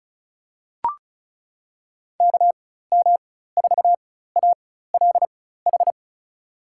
Morse code, heard as a keyed tone, about 36 wpm, spaced as for 16 wpm, 700 Hz. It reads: KM4ALH
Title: USA Call Signs - 6 Characters - 4x Character Spacing - 35wpm